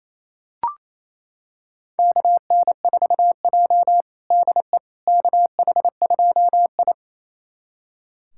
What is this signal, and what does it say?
Morse code, heard as a keyed tone, about 28 wpm, 700 Hz. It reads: KN4J DE KH2I
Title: USA Callsign DE Callsign - 4 Characters - 28wpm